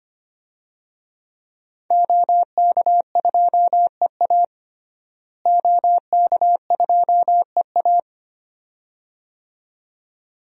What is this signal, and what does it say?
Morse code, heard as a keyed tone, about 25 wpm, 700 Hz. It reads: OK2EA OK2EA
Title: DXCC-100 Call Signs - 5 Characters - Repeated 2x - 25wpm